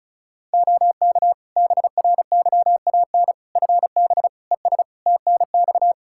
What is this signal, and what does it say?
Morse code, heard as a keyed tone, about 34 wpm, 700 Hz. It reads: OK BRYAN FB ES TNX
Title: Practice QSO - Exchange 2 - Info - 35wpm